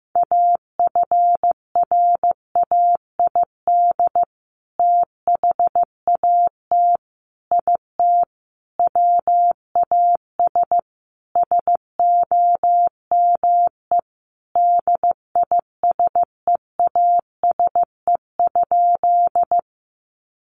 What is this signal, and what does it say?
Morse code, heard as a keyed tone, about 15 wpm, 700 Hz. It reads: AFRAID THAT IT WAS SOME DISEASE?
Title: Sentences from Top 1500 Words 15wpm